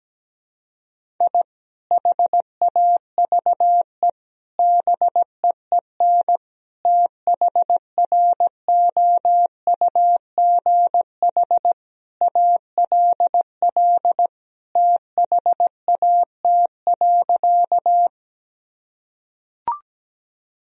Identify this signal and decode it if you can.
Morse code, heard as a keyed tone, about 17 wpm, 700 Hz. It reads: I HAVE BEEN THROUGH ALL THAT.
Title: Sentences from Top 1000 Words - Review Part 1 - 17wpm